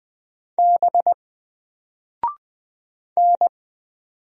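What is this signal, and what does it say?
Morse code, heard as a keyed tone, about 20 wpm, 700 Hz. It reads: B N
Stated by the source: Single letter and number Morse drill at 20 wpm, from N4LCD.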